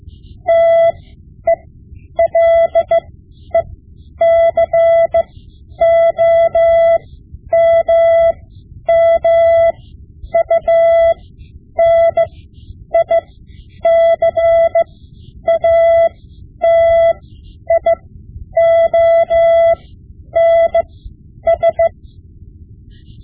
Morse code, heard as a keyed tone, about 15 wpm, spaced as for 9 wpm, 670 Hz. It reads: TELECOMMUNICATIONS
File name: Morse Lessons / Morse Lessons / Tape-5B